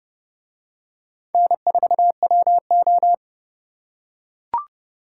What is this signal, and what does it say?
Morse code, heard as a keyed tone, about 28 wpm, 700 Hz. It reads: N4WO